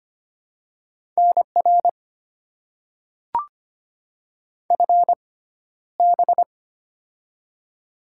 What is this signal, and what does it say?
Morse code, heard as a keyed tone, about 25 wpm, spaced as for 13 wpm, 700 Hz. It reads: NR FB
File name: Top 90 QSO Elements - 6x Character Spacing - 25wpm